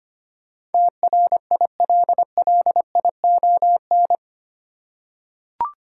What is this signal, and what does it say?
Morse code, heard as a keyed tone, about 25 wpm, 700 Hz. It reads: TRILLION
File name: 8 Letter Words - First 3 Letters Given - 25wpm